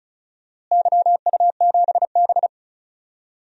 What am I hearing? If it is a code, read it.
Morse code, heard as a keyed tone, about 34 wpm, 700 Hz. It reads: YU7B
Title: DXCC-100 Call Signs - 4 Characters - Repeated 3x - 35wpm